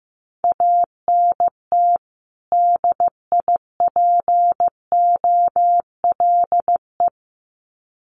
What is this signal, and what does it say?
Morse code, heard as a keyed tone, about 15 wpm, 700 Hz. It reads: ANT DIPOLE
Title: Common QSO Phrases 15wpm